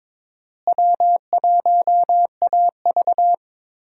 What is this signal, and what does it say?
Morse code, heard as a keyed tone, about 22 wpm, 700 Hz. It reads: W1AV
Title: USA Call Signs - 4 Characters - Repeated 2x - 22wpm